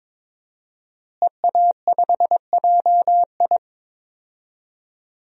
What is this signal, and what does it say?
Morse code, heard as a keyed tone, about 22 wpm, 700 Hz. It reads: EA5JI
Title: DXCC-100 Call Signs - 5 Characters - Repeated 3x - 22wpm